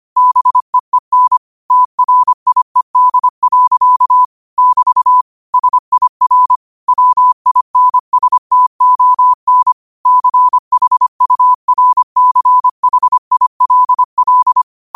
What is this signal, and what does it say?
Morse code, heard as a keyed tone, about 25 wpm, 1000 Hz. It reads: BEEN TRIED. <BT> SIR WINSTON CHURCHILL
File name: qotdfiles / 25wpm